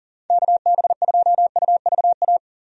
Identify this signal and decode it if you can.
Morse code, heard as a keyed tone, about 40 wpm, 700 Hz. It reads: KB2UVA
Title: USA Call Signs - 6 Characters - 40wpm